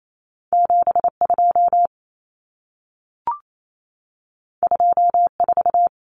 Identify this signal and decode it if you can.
Morse code, heard as a keyed tone, about 28 wpm, 700 Hz. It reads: 72 24
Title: Common Cut Numbers - 2-Digit 28wpm